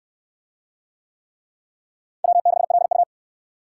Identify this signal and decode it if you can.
Morse code, heard as a keyed tone, about 63 wpm, 700 Hz. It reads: W6LU